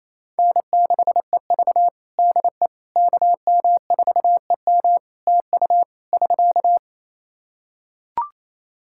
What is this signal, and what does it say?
Morse code, heard as a keyed tone, about 28 wpm, 700 Hz. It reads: N6EV DE KM4EM TU <SK>
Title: Practice QSO - Exchange 4 - Ending - 28wpm